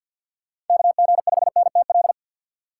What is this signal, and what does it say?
Morse code, heard as a keyed tone, about 51 wpm, 700 Hz. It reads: KG5NTL
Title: USA Call Signs - 6 Characters - 6x Character Spacing - 50wpm